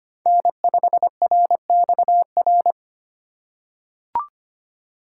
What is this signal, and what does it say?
Morse code, heard as a keyed tone, about 25 wpm, 700 Hz. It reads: N5RXR
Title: USA Call Signs - 5 Characters - 2x Character Spacing - 25wpm